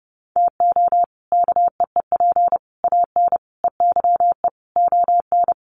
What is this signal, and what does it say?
Morse code, heard as a keyed tone, about 30 wpm, 700 Hz. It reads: TO KEEP AN EYE ON
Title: Sets of 5 Words Encore 30wpm